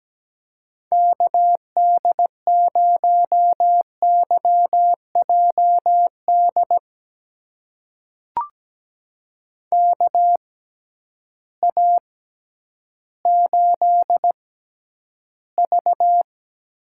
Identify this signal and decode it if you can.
Morse code, heard as a keyed tone, about 17 wpm, spaced as for 10 wpm, 700 Hz. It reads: KD0YJD KA8V
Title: USA Call Signs - 6x Character Spacing - 17wpm